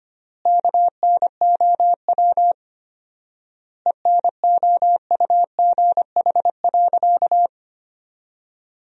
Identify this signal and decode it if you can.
Morse code, heard as a keyed tone, about 25 wpm, 700 Hz. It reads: KNOW ENOUGH.
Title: 4x Word-Spacing - Sentences from Top 400 Words - 25wpm